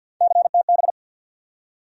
Morse code, heard as a keyed tone, about 47 wpm, 700 Hz. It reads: CTB
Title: USA Call Sign Postfixes - 3 Letters 50wpm